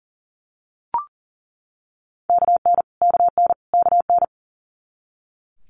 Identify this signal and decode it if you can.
Morse code, heard as a keyed tone, about 40 wpm, 690 Hz. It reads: KN KN KN
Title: Multi-letter Phonemes Repeated 3x 40wpm